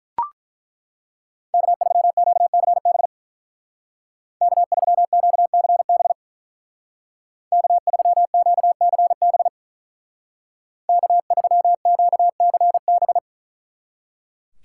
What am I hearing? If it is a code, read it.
Morse code, heard as a keyed tone, about 42 wpm, 700 Hz. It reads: K3QCB K3QCB K3QCB K3QCB
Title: Speed Racing - USA Call Signs 35wpm